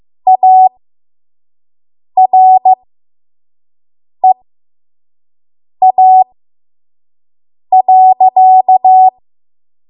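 Morse code, heard as a keyed tone, about 15 wpm, spaced as for 5 wpm, 750 Hz. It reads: AREA.
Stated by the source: 5 WPM Morse code practice.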